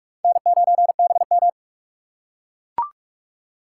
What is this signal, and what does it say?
Morse code, heard as a keyed tone, about 41 wpm, 700 Hz. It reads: N9BM
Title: USA Call Signs - 4 Characters - 3x Character Spacing - 45wpm